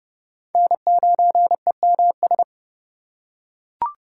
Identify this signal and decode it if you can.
Morse code, heard as a keyed tone, about 30 wpm, 700 Hz. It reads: N9EMS
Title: USA Call Signs - 5 Characters - 8x Character Spacing - 30wpm